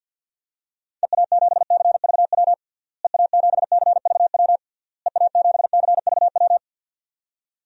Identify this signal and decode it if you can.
Morse code, heard as a keyed tone, about 52 wpm, 700 Hz. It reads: EA7CVW EA7CVW EA7CVW
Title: DXCC-100 Call Signs - 6 Characters - Repeated 3x - 50wpm